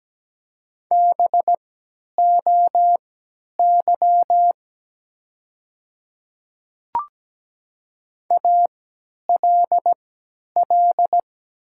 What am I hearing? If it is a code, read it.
Morse code, heard as a keyed tone, about 17 wpm, spaced as for 10 wpm, 700 Hz. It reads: BOY ALL
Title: 3x Character Spacing - Top 300 Words - 17wpm